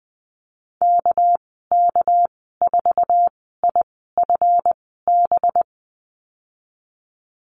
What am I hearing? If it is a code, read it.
Morse code, heard as a keyed tone, about 20 wpm, spaced as for 15 wpm, 700 Hz. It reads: KK4IFB
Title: 207 Lesson 33g - Letter j - Call Signs 20wpm